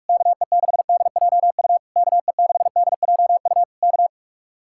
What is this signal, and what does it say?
Morse code, heard as a keyed tone, about 59 wpm, spaced as for 52 wpm, 700 Hz. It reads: KE6DJU KE6DJU K
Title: Speed-Racing Common QSO Phrases - 45wpm